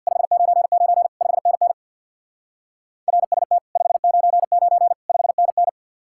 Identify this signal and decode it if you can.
Morse code, heard as a keyed tone, about 64 wpm, 700 Hz. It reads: 599 5NN RST 599 5NN